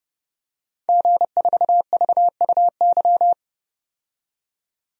Morse code, heard as a keyed tone, about 31 wpm, 700 Hz. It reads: G4VUY
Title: DXCC-100 Call Signs - 5 Characters - 30wpm